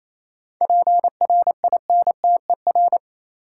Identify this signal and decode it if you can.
Morse code, heard as a keyed tone, about 27 wpm, 690 Hz. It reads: PRINTER
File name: Easy Words - Speed-Racing with Hints - 22wpm